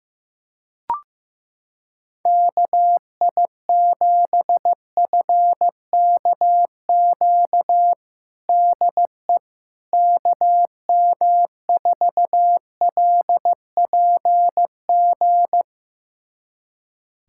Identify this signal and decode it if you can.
Morse code, heard as a keyed tone, about 16 wpm, 700 Hz. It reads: KI7FKQ DE KM4LPG
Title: USA Callsign DE Callsign - 6 Characters - 15wpm